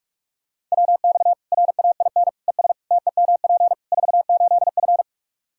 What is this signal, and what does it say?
Morse code, heard as a keyed tone, about 42 wpm, 700 Hz. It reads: WX RAIN ES TEMP 48F
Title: Practice QSO - Exchange 2 - Info - 45wpm